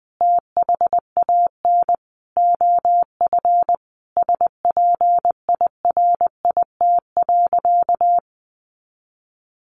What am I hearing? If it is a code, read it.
Morse code, heard as a keyed tone, about 20 wpm, 700 Hz. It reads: THAN OF SPIRIT.
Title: Sentences from Top 1400 Words 20wpm